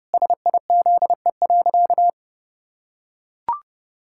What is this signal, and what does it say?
Morse code, heard as a keyed tone, about 30 wpm, 700 Hz. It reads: SIZE.
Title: Pangram Challenge v2 - 30wpm